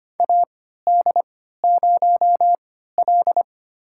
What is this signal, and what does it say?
Morse code, heard as a keyed tone, about 25 wpm, spaced as for 14 wpm, 700 Hz. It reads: AD0L
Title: USA Call Signs - 4 Characters - 3x Character Spacing - 25wpm